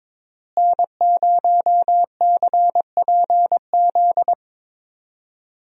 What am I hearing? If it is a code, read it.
Morse code, heard as a keyed tone, about 22 wpm, 700 Hz. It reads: N0CPZ
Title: USA Call Signs - 5 Characters - Repeated 3x - 22wpm